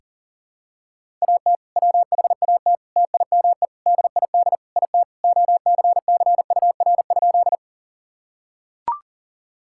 Morse code, heard as a keyed tone, about 40 wpm, 700 Hz. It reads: AT WHAT TIME DID IT OCCUR?
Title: Sentences from Top 1000 Words - Review Part 6 - 40wpm